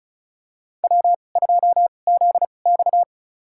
Morse code, heard as a keyed tone, about 34 wpm, spaced as for 25 wpm, 700 Hz. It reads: W2ZX
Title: USA Call Signs - 2x Character Spacing - 35wpm